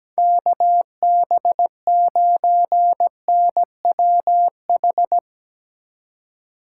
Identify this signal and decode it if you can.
Morse code, heard as a keyed tone, about 17 wpm, 700 Hz. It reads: KB9NWH
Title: Speed Racing - USA 2x3 Call Signs 17wpm